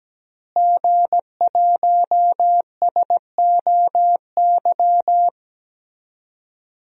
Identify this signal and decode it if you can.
Morse code, heard as a keyed tone, about 17 wpm, 700 Hz. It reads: G1SOY
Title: DXCC-100 Call Signs - 5 Characters - Repeated 2x - 17wpm